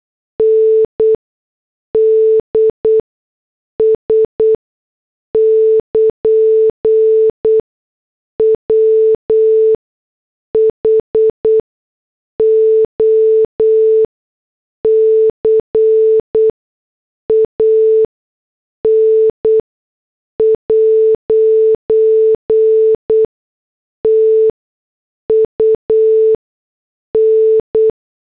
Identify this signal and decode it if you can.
Morse code, heard as a keyed tone, about 8 wpm, 440 Hz. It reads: N D S <KN> W H O C A N ' T U N
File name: morse / headlines